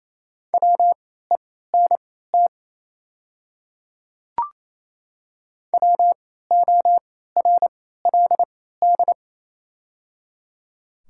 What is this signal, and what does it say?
Morse code, heard as a keyed tone, about 28 wpm, spaced as for 16 wpm, 700 Hz. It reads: WENT WORLD
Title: 3x Character Spacing - Top 200 Words - 28wpm